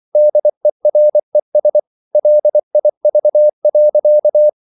Morse code, heard as a keyed tone, about 24 wpm, 600 Hz. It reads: DERES LIV.
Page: Ferie 24 wpm